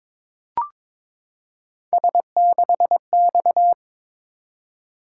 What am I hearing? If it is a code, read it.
Morse code, heard as a keyed tone, about 22 wpm, 700 Hz. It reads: S6X
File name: practice / 22wpm / Three Letter-Number Combinations - 22wpm